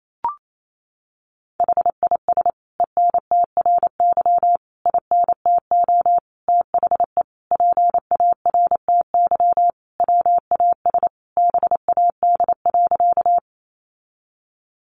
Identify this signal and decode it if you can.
Morse code, heard as a keyed tone, about 28 wpm, 700 Hz. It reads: HIS ENTRY INTO THE PARTY WAS BAD.
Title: Sentences from Top 1500 Words 28wpm